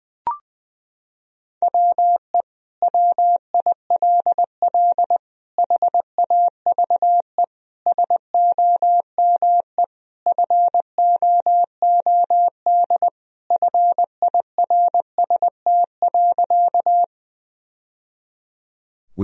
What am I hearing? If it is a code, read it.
Morse code, heard as a keyed tone, about 20 wpm, 700 Hz. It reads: WE WILL HAVE SOME FOOD FIRST.
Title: Sentences from Top 1000 Words - Review Part 1 - 20wpm